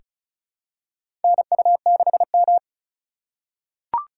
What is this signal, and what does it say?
Morse code, heard as a keyed tone, about 36 wpm, 700 Hz. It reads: NU6M
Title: USA Call Signs - 4 Characters - 4x Character Spacing - 35wpm